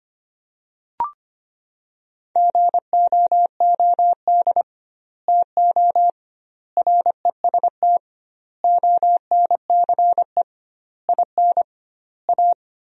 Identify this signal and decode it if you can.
Morse code, heard as a keyed tone, about 25 wpm, 700 Hz. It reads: GOOD TO REST ONCE IN A
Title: 2x Word-Spacing - Sentences from Top 500 Words - 25wpm